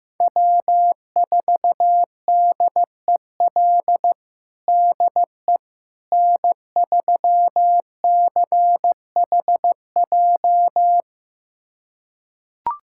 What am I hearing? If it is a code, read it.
Morse code, heard as a keyed tone, about 15 wpm, 700 Hz. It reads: W4DEL DE N3CHJ
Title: USA Callsign DE Callsign - 5 Characters - 15wpm